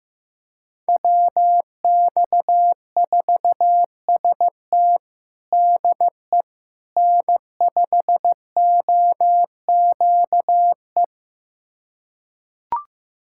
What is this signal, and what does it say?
Morse code, heard as a keyed tone, about 15 wpm, 700 Hz. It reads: WX4ST DE N5OQE
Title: USA Callsign DE Callsign 15wpm